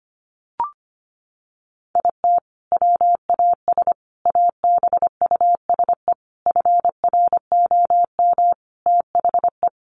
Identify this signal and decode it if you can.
Morse code, heard as a keyed tone, about 25 wpm, 700 Hz. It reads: IT WAS ABUSE FROM THE
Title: Sentences from Top 1300 Words 25wpm